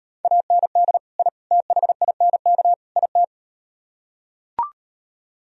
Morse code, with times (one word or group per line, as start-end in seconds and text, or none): 0.2-1.0 AND
1.2-1.3 I
1.5-2.7 THINK
3.0-3.2 IT